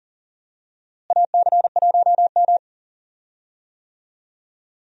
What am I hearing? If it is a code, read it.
Morse code, heard as a keyed tone, about 39 wpm, 700 Hz. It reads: AC1M
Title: USA Call Signs - 4 Characters - Repeated 3x - 40wpm